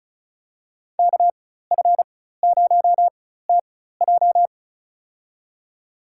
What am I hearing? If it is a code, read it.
Morse code, heard as a keyed tone, about 34 wpm, spaced as for 16 wpm, 700 Hz. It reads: KF0TJ
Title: USA Call Signs - 5 Characters - 4x Character Spacing - 35wpm